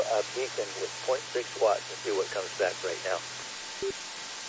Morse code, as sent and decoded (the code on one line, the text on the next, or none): none